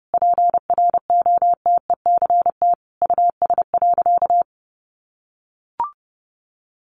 Morse code, heard as a keyed tone, about 30 wpm, 700 Hz. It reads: PROTECT US.
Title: Sentences from Top 900 Words 30wpm